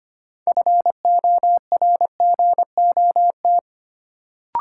Morse code, heard as a keyed tone, about 25 wpm, 700 Hz. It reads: FORGOT
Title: Sentences from Top 1400 Words - New Word Drill Alt - 25wpm